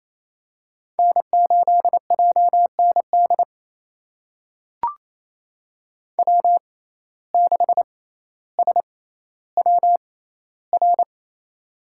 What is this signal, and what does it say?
Morse code, heard as a keyed tone, about 28 wpm, spaced as for 14 wpm, 700 Hz. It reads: N8JND W6SWR
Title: USA Call Signs - 5 Characters - 6x Character Spacing - 28wpm